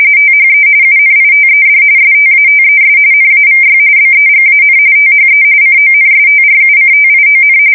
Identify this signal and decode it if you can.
RTTY, about 45.45 baud. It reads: THE UNITED STATES NAVY BEGAN THE CONSTRUCTION
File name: fsk170 / battleships